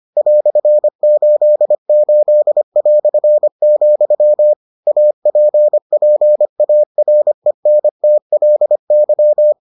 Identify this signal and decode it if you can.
Morse code, heard as a keyed tone, about 25 wpm, 600 Hz. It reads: "88", APPARENTLY